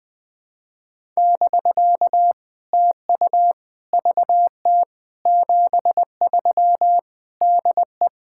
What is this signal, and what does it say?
Morse code, heard as a keyed tone, about 20 wpm, 700 Hz. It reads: <BK> TU VT 73 DE
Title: POTA QSO Practice - 20wpm